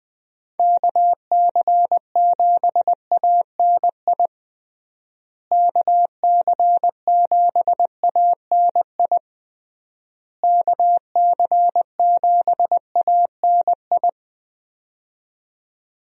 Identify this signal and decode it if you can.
Morse code, heard as a keyed tone, about 21 wpm, 700 Hz. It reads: KC7ANI KC7ANI KC7ANI
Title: USA Call Signs - 6 Characters - Repeated 3x - 20wpm